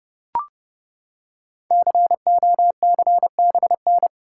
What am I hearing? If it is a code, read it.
Morse code, heard as a keyed tone, about 30 wpm, 700 Hz. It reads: COCBN
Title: Code Groups - 5 Letters - 30wpm